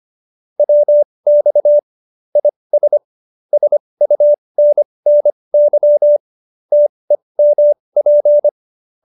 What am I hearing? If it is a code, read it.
Morse code, heard as a keyed tone, about 25 wpm, spaced as for 20 wpm, 590 Hz. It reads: WX IS SUNNY TEMP
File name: QSO310 20